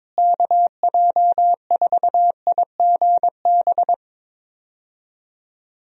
Morse code, heard as a keyed tone, about 22 wpm, 700 Hz. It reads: KJ4IGB
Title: USA Call Signs - 6 Characters - Repeated 3x - 22wpm